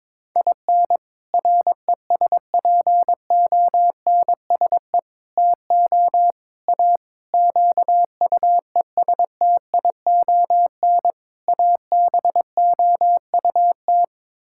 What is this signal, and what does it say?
Morse code, heard as a keyed tone, about 22 wpm, 700 Hz. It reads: IN RESPONSE TO A QUESTION ABOUT
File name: practice / 22wpm / Sets of 6 Words - 22wpm